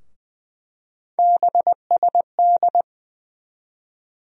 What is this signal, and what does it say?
Morse code, heard as a keyed tone, about 20 wpm, 700 Hz. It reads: BSD
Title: USA Call Sign Postfixes - 3 Letters 20wpm